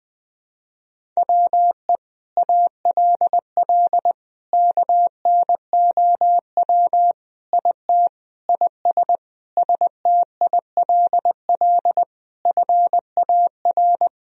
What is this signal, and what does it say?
Morse code, heard as a keyed tone, about 20 wpm, 700 Hz. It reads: WE ALL KNOW IT IS STILL FAR
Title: Sentences from Top 1000 Words - Review Part 1 - 20wpm